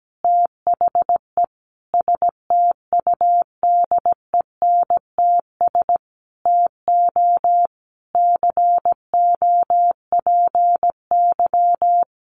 Morse code, heard as a keyed tone, about 17 wpm, 700 Hz. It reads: THE STUDENTS TO COPY
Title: Additional Easy Phrases 17wpm